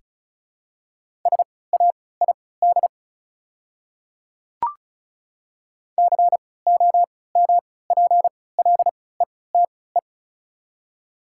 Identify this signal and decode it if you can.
Morse code, heard as a keyed tone, about 36 wpm, spaced as for 20 wpm, 700 Hz. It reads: SAID COMPLETE
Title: 3x Character Spacing - Top 500 Words - 35wpm